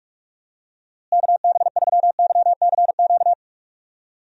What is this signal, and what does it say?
Morse code, heard as a keyed tone, about 46 wpm, 700 Hz. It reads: KB3YCQ
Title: USA Call Signs - 6 Characters - Repeated 3x - 45wpm